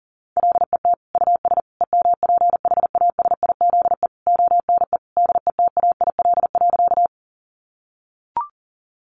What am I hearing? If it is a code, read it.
Morse code, heard as a keyed tone, about 37 wpm, 700 Hz. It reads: LET US EMPHASIZE ONE DETAIL.
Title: Sentences from Top 1500 Words 40wpm